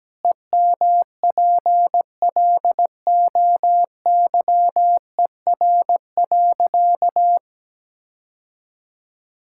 Morse code, heard as a keyed tone, about 17 wpm, 700 Hz. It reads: EMPLOYER.
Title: Sentences from Top 1600 Words 17wpm